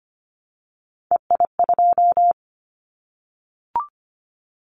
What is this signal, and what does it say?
Morse code, heard as a keyed tone, about 25 wpm, 700 Hz. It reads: EI2
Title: Callsign Prefixes for Top 50 DXCC - 25wpm